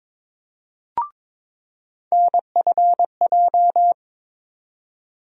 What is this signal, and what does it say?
Morse code, heard as a keyed tone, about 22 wpm, 700 Hz. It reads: NFJ
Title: USA Call Sign Postfixes - 3 Letters 22wpm